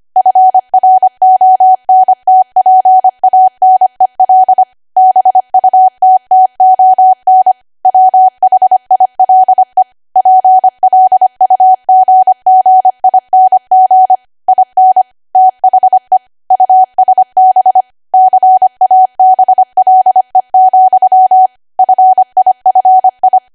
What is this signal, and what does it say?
Morse code, heard as a keyed tone, about 25 wpm, 750 Hz. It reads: FRONTPANEL BUTTON WHILE PLUGGING IN THE USB CABLE, FIFI
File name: cwa / W1AW-Level3 / 25WPM